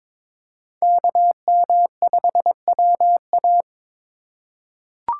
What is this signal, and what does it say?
Morse code, heard as a keyed tone, about 21 wpm, 700 Hz. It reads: KM5WA
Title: USA Call Signs - 5 Characters - 3x Character Spacing - 22wpm